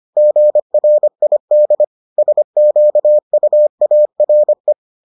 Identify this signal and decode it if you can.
Morse code, heard as a keyed tone, about 25 wpm, 600 Hz. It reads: GRID SQUARE